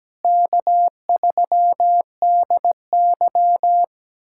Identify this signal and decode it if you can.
Morse code, heard as a keyed tone, about 17 wpm, 700 Hz. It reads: K3DY
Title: USA Callsign DE Callsign - 4 Characters - 17wpm